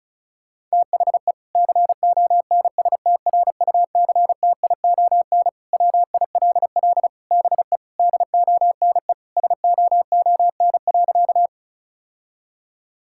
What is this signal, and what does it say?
Morse code, heard as a keyed tone, about 36 wpm, 700 Hz. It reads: THE CONSTRUCTION WILL BE DONE SOON.
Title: Sentences from Top 1600 Words 35wpm